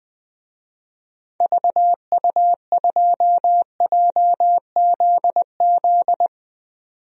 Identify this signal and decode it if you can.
Morse code, heard as a keyed tone, about 20 wpm, 700 Hz. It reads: VU2JZZ
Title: DXCC-100 Call Signs - 6 Characters - 20wpm